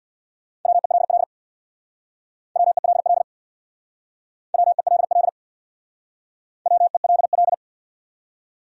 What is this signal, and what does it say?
Morse code, heard as a keyed tone, about 68 wpm, 700 Hz. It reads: WELL WELL WELL WELL